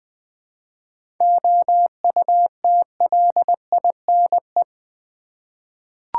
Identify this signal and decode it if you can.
Morse code, heard as a keyed tone, about 20 wpm, 700 Hz. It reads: OUTLINE
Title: 7 Letter Words - First 3 Letters Given - 20wpm